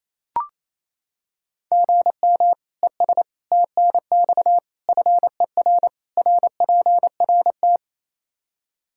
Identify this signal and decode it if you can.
Morse code, heard as a keyed tone, about 28 wpm, 700 Hz. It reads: GM ES TNX FER RPRT
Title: K1USN SST QSO Practice 28wpm